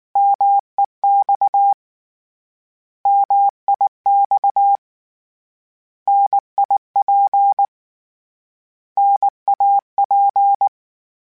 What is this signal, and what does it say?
Morse code, heard as a keyed tone, about 19 wpm, 800 Hz. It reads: MEX MIX NIP NAP
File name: Contrasts / Contrasts-mp3s / 3LtrContrasts @18